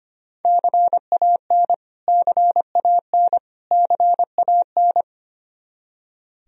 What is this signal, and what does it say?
Morse code, heard as a keyed tone, about 25 wpm, 700 Hz. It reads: CAN CAN CAN
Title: Top 100 Words Repeated 25wpm